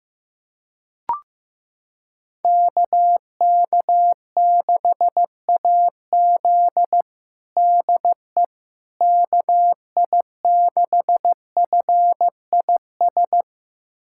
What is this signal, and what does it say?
Morse code, heard as a keyed tone, about 15 wpm, 700 Hz. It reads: KK6AZ DE KI6FIS
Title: USA Callsign DE Callsign 15wpm